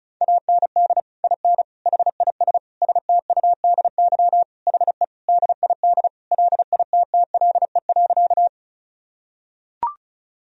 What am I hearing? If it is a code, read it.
Morse code, heard as a keyed tone, about 34 wpm, 700 Hz. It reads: AND IN HIS STUDY HE DID LITTLE.
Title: Sentences from Top 1000 Words - Review Part 1 - 35wpm